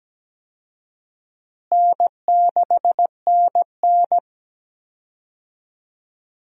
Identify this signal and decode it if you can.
Morse code, heard as a keyed tone, about 17 wpm, 700 Hz. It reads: N6NN
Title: USA Call Signs - 4 Characters - Repeated 2x - 17wpm